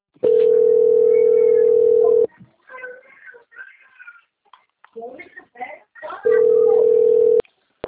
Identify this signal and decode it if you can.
Ringback tone (North America), 440 + 480 Hz, with speech.
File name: out- - - - - .54369